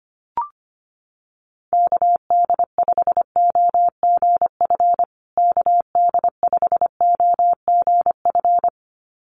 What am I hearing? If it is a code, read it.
Morse code, heard as a keyed tone, about 25 wpm, 700 Hz. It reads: KD5OGF KD5OGF
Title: USA Call Signs Repeated 2x 25wpm